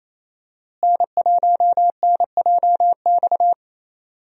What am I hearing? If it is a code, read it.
Morse code, heard as a keyed tone, about 28 wpm, 700 Hz. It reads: N1NJX
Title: USA Call Signs - 5 Characters - Repeated 2x - 28wpm